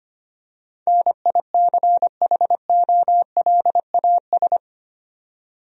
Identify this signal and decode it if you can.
Morse code, heard as a keyed tone, about 25 wpm, 700 Hz. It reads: NICHOLAS
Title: USA Common Names - 25wpm